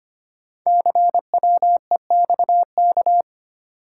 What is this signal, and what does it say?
Morse code, heard as a keyed tone, about 25 wpm, 700 Hz. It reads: CWEXK
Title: Code Groups - 5 Letters - 25wpm